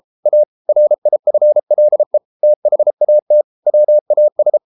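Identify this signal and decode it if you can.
Morse code, heard as a keyed tone, about 33 wpm, 600 Hz. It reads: A RIFLE THAT WAS
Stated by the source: Morse code practice at 32 wpm from CW med Gnister, reading Ken Follett's novel Never in English.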